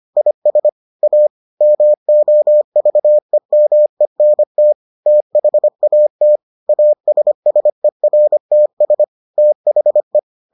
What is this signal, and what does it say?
Morse code, heard as a keyed tone, about 25 wpm, 600 Hz. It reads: IS A MOVEMENT THAT ASSERTS THE